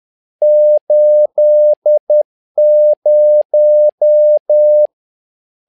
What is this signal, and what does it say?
Morse code, heard as a keyed tone, about 10 wpm, 600 Hz. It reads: 80
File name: learning center / tower trouble / tower trouble 10wpm